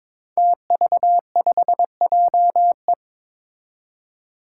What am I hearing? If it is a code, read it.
Morse code, heard as a keyed tone, about 22 wpm, 700 Hz. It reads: TV5JE
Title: Five Challenging Letter-Number Combinations 22wpm